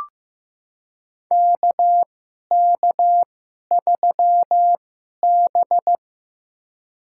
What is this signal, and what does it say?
Morse code, heard as a keyed tone, about 15 wpm, spaced as for 11 wpm, 700 Hz. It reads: KK3B
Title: USA Call Signs - 4 Characters - 2x Character Spacing - 15wpm